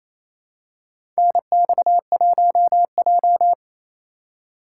Morse code, heard as a keyed tone, about 28 wpm, 700 Hz. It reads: NX1J
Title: USA Call Signs - 4 Characters - 3x Character Spacing - 28wpm